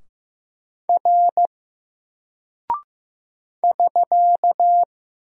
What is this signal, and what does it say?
Morse code, heard as a keyed tone, about 15 wpm, 700 Hz. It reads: R <SK>
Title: Single Character (Letter:Number:Punctuation:Prosign) 15wpm